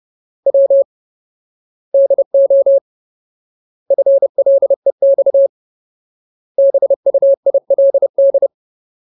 Morse code, heard as a keyed tone, about 30 wpm, 550 Hz. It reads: W DO FLEX BUILD
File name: Ultimate Word List / 30WPM Ultimate